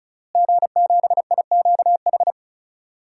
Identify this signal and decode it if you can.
Morse code, heard as a keyed tone, about 36 wpm, 700 Hz. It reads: G7IQH
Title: DXCC-100 Call Signs - 5 Characters - Repeated 2x - 35wpm